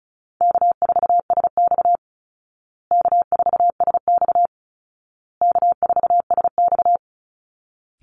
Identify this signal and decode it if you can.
Morse code, heard as a keyed tone, about 34 wpm, 700 Hz. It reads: K4SX K4SX K4SX
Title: Call Signs Repeated 2x - 35wpm